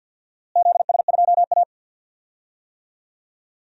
Morse code, heard as a keyed tone, about 52 wpm, 700 Hz. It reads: ZS2A
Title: DXCC-100 Call Signs - 4 Characters - Repeated 2x - 50wpm